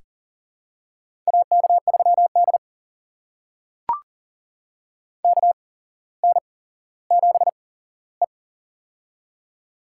Morse code, heard as a keyed tone, about 42 wpm, spaced as for 17 wpm, 700 Hz. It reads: AK3D KN7E
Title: USA Call Signs - 4 Characters - 8x Character Spacing - 40wpm